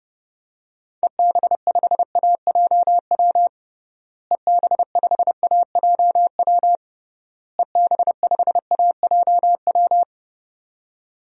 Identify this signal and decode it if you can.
Morse code, heard as a keyed tone, about 30 wpm, 700 Hz. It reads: EB5AJW EB5AJW EB5AJW
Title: DXCC-100 Call Signs - 6 Characters - Repeated 3x - 30wpm